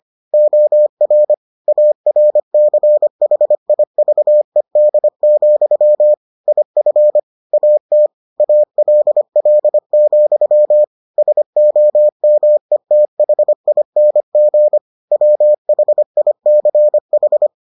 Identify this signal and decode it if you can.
Morse code, heard as a keyed tone, about 25 wpm, 600 Hz. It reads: OR ARCHIVED, IF AT ALL, SOMETHING WHICH